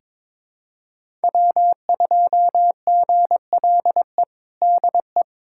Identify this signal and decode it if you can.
Morse code, heard as a keyed tone, about 22 wpm, 700 Hz. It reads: W2GLE DE
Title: USA Callsign DE Callsign 22wpm